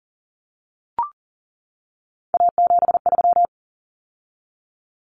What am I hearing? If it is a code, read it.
Morse code, heard as a keyed tone, about 40 wpm, 700 Hz. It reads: A73
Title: Common Cut Numbers - 3-Digit 40wpm